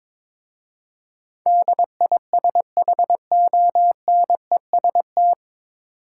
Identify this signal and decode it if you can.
Morse code, heard as a keyed tone, about 22 wpm, 700 Hz. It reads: DISHONEST
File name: 9 Letter Words - First 3 Letters Given - 22wpm